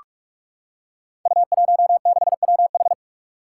Morse code, heard as a keyed tone, about 44 wpm, 700 Hz. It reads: U16WH
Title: Five Challenging Letter-Number Combinations 45wpm